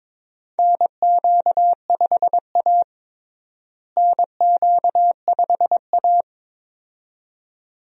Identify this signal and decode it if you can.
Morse code, heard as a keyed tone, about 22 wpm, 700 Hz. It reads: NQ5A NQ5A
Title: USA Call Signs - 4 Characters - Repeated 2x - 22wpm